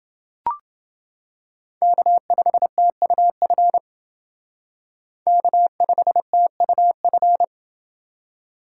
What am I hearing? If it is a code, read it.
Morse code, heard as a keyed tone, about 29 wpm, 700 Hz. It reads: K5TUF K5TUF